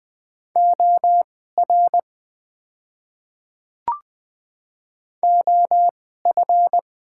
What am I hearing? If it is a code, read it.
Morse code, heard as a keyed tone, about 20 wpm, spaced as for 14 wpm, 690 Hz. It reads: OR OF
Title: 2x Character Spacing - Top 100 Words - 20wpm